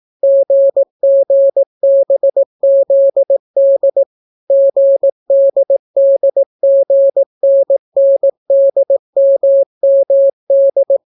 Morse code, heard as a keyed tone, about 18 wpm, 550 Hz. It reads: GGBZD GDDGNNDMMD